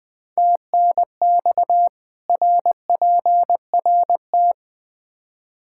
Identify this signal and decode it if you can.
Morse code, heard as a keyed tone, about 20 wpm, 700 Hz. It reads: TNX RPRT
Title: Practice QSO - Exchange 2 - Info - 20wpm